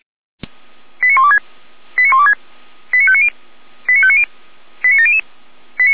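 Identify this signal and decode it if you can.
Five-tone selective call (ZVEI), heard as quick runs of five tones, tones of about 70 ms. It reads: 89126 89126 89590 89590 89700
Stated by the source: ZVEI five-tone selective call received by SDR, archived on server of feuerwehr-orlamuende.org (Orlamünde fire brigade).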